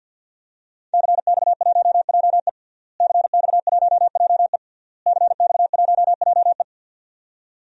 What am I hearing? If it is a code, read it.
Morse code, heard as a keyed tone, about 52 wpm, 700 Hz. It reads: CX1JE CX1JE CX1JE
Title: DXCC-100 Call Signs - 5 Characters - Repeated 3x - 50wpm